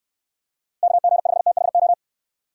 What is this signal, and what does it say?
Morse code, heard as a keyed tone, about 67 wpm, 700 Hz. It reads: KG5THO